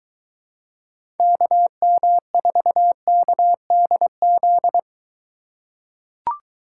Morse code, heard as a keyed tone, about 23 wpm, 700 Hz. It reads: KM4KDZ